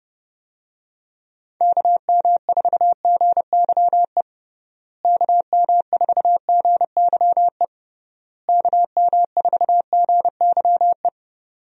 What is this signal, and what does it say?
Morse code, heard as a keyed tone, about 32 wpm, 700 Hz. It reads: KM4GYE KM4GYE KM4GYE
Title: USA Call Signs - 6 Characters - Repeated 3x - 30wpm